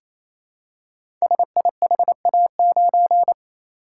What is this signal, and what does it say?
Morse code, heard as a keyed tone, about 27 wpm, 700 Hz. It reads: SIHA9
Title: Code Groups - 5 Letter:Numbers - 28wpm